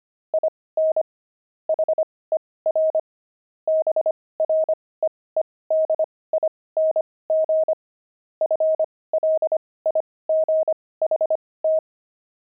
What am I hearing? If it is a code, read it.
Morse code, heard as a keyed tone, about 25 wpm, spaced as for 18 wpm, 630 Hz. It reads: IN HER BREEDING FLIGHT
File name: pangram-fs-18-cs-25-beehive